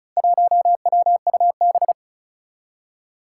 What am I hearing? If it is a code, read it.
Morse code, heard as a keyed tone, about 36 wpm, 700 Hz. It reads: 1WUB
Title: Four Challenging Letter-Number Combinations 35wpm